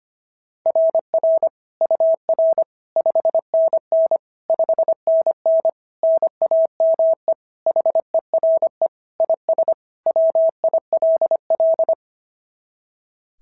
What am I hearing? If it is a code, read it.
Morse code, heard as a keyed tone, about 24 wpm, 650 Hz. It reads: RR UR 5NN 5NN NAME HERE IS WILL
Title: SOTA QSO Practice 25wpm